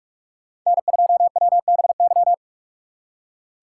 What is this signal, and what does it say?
Morse code, heard as a keyed tone, about 44 wpm, 700 Hz. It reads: N2WBY
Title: USA Call Signs - 5 Characters - 3x Character Spacing - 45wpm